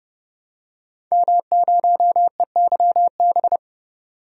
Morse code, heard as a keyed tone, about 31 wpm, 700 Hz. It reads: M0EYB
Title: DXCC-100 Call Signs - 5 Characters - 30wpm